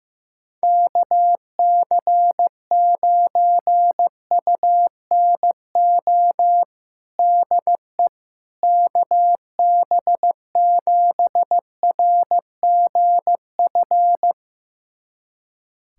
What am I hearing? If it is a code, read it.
Morse code, heard as a keyed tone, about 15 wpm, 700 Hz. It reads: KC9UNO DE KB7RGF
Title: USA Callsign DE Callsign 15wpm